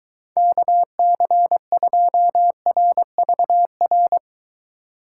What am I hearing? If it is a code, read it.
Morse code, heard as a keyed tone, about 23 wpm, 700 Hz. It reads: KC2RVR